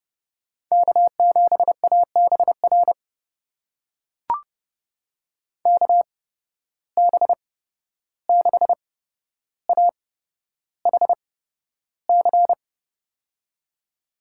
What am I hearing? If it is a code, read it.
Morse code, heard as a keyed tone, about 30 wpm, spaced as for 12 wpm, 700 Hz. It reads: K7ABR KB6AHC
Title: USA Call Signs - 8x Character Spacing - 30wpm